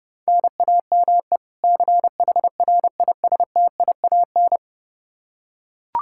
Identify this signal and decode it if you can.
Morse code, heard as a keyed tone, about 32 wpm, 700 Hz. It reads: NAME CHRISTIAN
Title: K1USN SST QSO Practice 30wpm